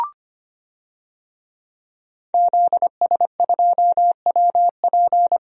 Morse code, heard as a keyed tone, about 25 wpm, 700 Hz. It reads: ZS2WP
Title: DXCC-100 Call Signs - 5 Characters - Repeated 3x - 25wpm